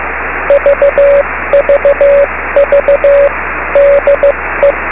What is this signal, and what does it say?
Morse code, heard as a keyed tone, about 15 wpm, 570 Hz, with radio noise. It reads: VVV DE
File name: beacons / k9mu